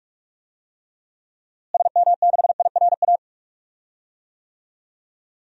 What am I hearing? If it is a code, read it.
Morse code, heard as a keyed tone, about 46 wpm, 700 Hz. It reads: SM6IRA
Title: DXCC-100 Call Signs - 6 Characters - Repeated 3x - 45wpm